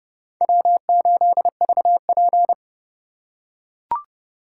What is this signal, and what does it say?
Morse code, heard as a keyed tone, about 28 wpm, 700 Hz. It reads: W8VP